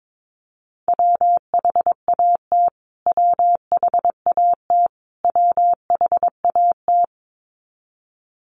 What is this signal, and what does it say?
Morse code, heard as a keyed tone, about 22 wpm, 700 Hz. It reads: WHAT WHAT WHAT
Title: Top 500 Words Repeated 22wpm